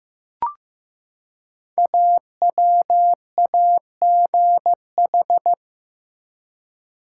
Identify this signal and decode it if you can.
Morse code, heard as a keyed tone, about 15 wpm, 700 Hz. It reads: AWAGH
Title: Code Groups - 5 Letter:Numbers - 15wpm